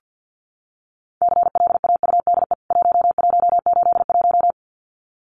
Morse code, heard as a keyed tone, about 48 wpm, 700 Hz. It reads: CLAUDE 1181
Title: CWT Exchange 50wpm